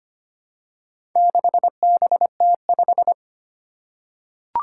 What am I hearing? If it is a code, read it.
Morse code, heard as a keyed tone, about 25 wpm, 700 Hz. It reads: 6BT5
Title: Four Challenging Letter-Number Combinations 25wpm